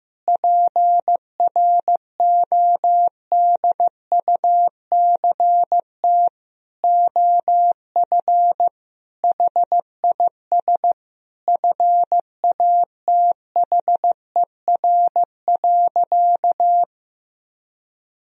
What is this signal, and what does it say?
Morse code, heard as a keyed tone, about 15 wpm, 700 Hz. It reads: PRODUCT OF HIS FATHER.
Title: Sentences from Top 1000 Words - Review Part 2 - 15wpm